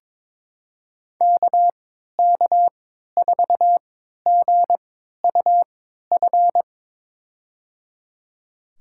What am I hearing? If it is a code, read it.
Morse code, heard as a keyed tone, about 22 wpm, spaced as for 12 wpm, 700 Hz. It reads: KK4GUF
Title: USA Call Signs - 6 Characters - 3x Character Spacing - 22wpm